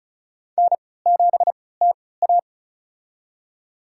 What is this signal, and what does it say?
Morse code, heard as a keyed tone, about 34 wpm, spaced as for 20 wpm, 700 Hz. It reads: N7TA
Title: USA Call Signs - 4 Characters - 3x Character Spacing - 35wpm